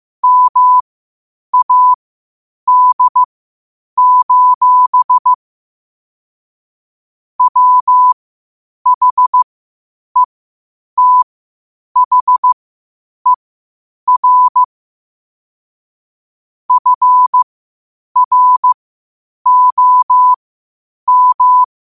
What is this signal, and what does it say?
Morse code, heard as a keyed tone, about 15 wpm, spaced as for 8 wpm, 1000 Hz. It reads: MAD: WHETHER FROM
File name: qotdfiles / 8wpm